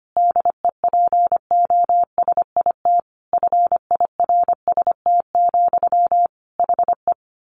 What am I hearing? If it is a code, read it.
Morse code, heard as a keyed tone, about 25 wpm, 700 Hz. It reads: DEPOSIT FIRST, HE
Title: Sentences from Top 1300 Words 25wpm